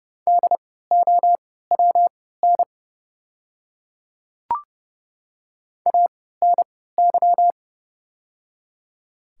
Morse code, heard as a keyed tone, about 31 wpm, spaced as for 17 wpm, 700 Hz. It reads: DOWN ANY
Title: 3x Character Spacing - Top 200 Words - 30wpm